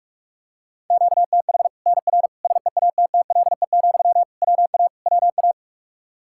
Morse code, heard as a keyed tone, about 46 wpm, 700 Hz. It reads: QTH NR SEATTLE, WA WA
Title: Practice QSO - Exchange 1 - Beginning - 45wpm